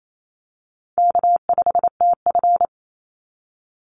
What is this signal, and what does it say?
Morse code, heard as a keyed tone, about 28 wpm, 700 Hz. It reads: K5TF
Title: Call Signs Repeated 2x - 28wpm